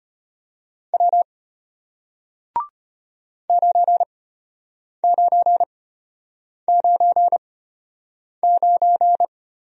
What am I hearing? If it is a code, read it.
Morse code, heard as a keyed tone, about 33 wpm, spaced as for 8 wpm, 700 Hz. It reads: W 9999